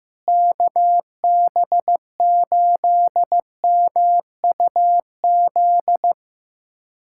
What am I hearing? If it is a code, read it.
Morse code, heard as a keyed tone, about 15 wpm, 700 Hz. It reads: KB8MUZ
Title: USA Callsign DE Callsign 15wpm